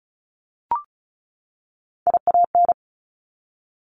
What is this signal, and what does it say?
Morse code, heard as a keyed tone, about 36 wpm, 700 Hz. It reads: IAN
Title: Top Three Letter Combinations in Words 35wpm